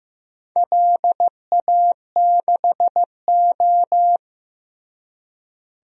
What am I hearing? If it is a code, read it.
Morse code, heard as a keyed tone, about 15 wpm, 700 Hz. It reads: LA6O
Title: DXCC-100 Call Signs - 4 Characters - Repeated 3x - 15wpm